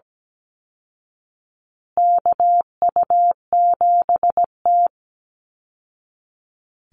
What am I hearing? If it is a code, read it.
Morse code, heard as a keyed tone, about 17 wpm, 700 Hz. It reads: KU7T
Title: Call Signs Repeated 3x - 17wpm